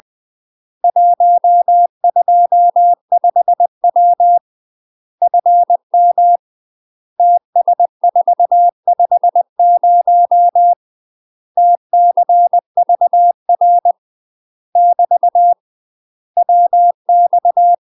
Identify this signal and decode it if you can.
Morse code, heard as a keyed tone, about 20 wpm, 700 Hz. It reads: 125W FM TS450 TCVR <BT> WX
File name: UK MASTER QSOs TRAINING / 20WPM / UK MASTER QSOs 0026